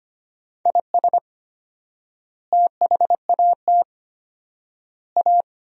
Morse code, heard as a keyed tone, about 25 wpm, 700 Hz. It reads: IS THAT A
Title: 4x Word-Spacing - Sentences from Top 500 Words - 25wpm